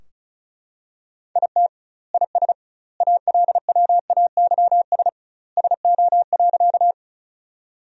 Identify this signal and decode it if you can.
Morse code, heard as a keyed tone, about 34 wpm, 700 Hz. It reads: IT IS ALWAYS SO.
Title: 2x Word-Spacing - Sentences from Top 300 Words - 35wpm